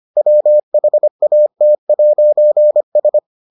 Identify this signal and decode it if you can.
Morse code, heard as a keyed tone, about 25 wpm, 600 Hz. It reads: WHAT'S